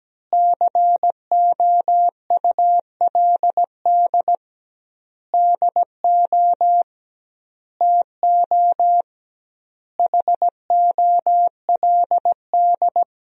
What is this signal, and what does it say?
Morse code, heard as a keyed tone, about 17 wpm, 700 Hz. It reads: COULD DO TO HOLD
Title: 2x Word-Spacing - Sentences from Top 400 Words - 17wpm